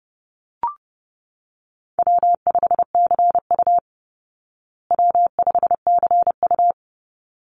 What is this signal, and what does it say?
Morse code, heard as a keyed tone, about 30 wpm, 700 Hz. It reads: W5CU W5CU
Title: Easy Call Signs Repeated 2x - 30wpm